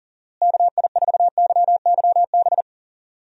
Morse code, heard as a keyed tone, about 40 wpm, 700 Hz. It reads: KI4YYB
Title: USA Call Signs - 6 Characters - Repeated 3x - 40wpm